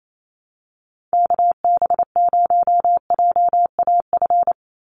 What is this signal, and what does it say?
Morse code, heard as a keyed tone, about 28 wpm, 700 Hz. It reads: KB0JAF
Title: USA Call Signs 28wpm